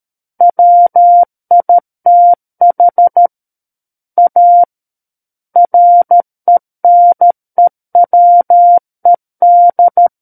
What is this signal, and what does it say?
Morse code, heard as a keyed tone, about 13 wpm, 700 Hz. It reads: WITH A RENEWED
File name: Book One / 04-Chapter 4 - 13 WPM